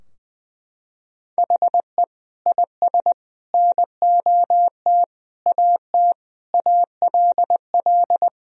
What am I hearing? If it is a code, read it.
Morse code, heard as a keyed tone, about 20 wpm, 700 Hz. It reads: HE IS NOT AT ALL
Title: Sentences from Top 1000 Words - Review Part 4 - 20wpm